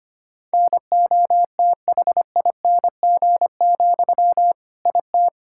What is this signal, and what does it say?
Morse code, heard as a keyed tone, about 25 wpm, 700 Hz. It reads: NOTHING, IT
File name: practice / 25wpm / All-Age Appropriate Jokes - 25wpm